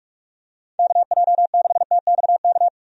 Morse code, heard as a keyed tone, about 46 wpm, 700 Hz. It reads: KJ6TXK
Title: USA Call Signs - 6 Characters - 45wpm